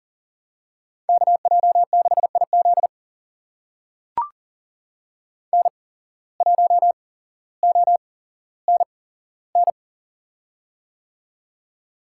Morse code, heard as a keyed tone, about 40 wpm, spaced as for 17 wpm, 700 Hz. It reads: KJ6IZ N1ONN
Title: USA Call Signs - 5 Characters - 8x Character Spacing - 40wpm